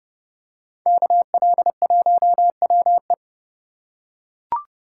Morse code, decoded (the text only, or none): KL1WE